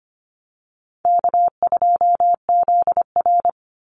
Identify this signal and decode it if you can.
Morse code, heard as a keyed tone, about 25 wpm, 700 Hz. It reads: K2ZR
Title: Easy Call Signs Repeated 2x - 25wpm